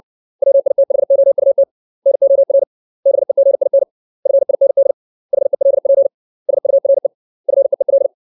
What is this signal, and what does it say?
Morse code, heard as a keyed tone, about 60 wpm, 550 Hz. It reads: WITHOUT NOR BEGIN FIND HELP SURE FEEL
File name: 632 words / 632 Words 60WPM